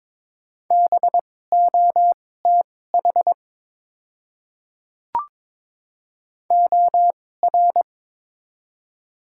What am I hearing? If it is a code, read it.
Morse code, heard as a keyed tone, about 22 wpm, spaced as for 16 wpm, 700 Hz. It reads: BOTH OR
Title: 2x Character Spacing - Top 300 Words - 22wpm